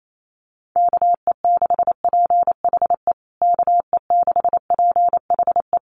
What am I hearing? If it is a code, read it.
Morse code, heard as a keyed tone, about 28 wpm, 700 Hz. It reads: KE6PHE KE6PHE
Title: USA Call Signs Repeated 3x 28wpm